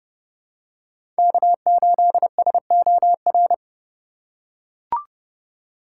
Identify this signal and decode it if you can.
Morse code, heard as a keyed tone, about 30 wpm, 700 Hz. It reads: K8SOR
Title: USA Call Signs - 5 Characters - 2x Character Spacing - 30wpm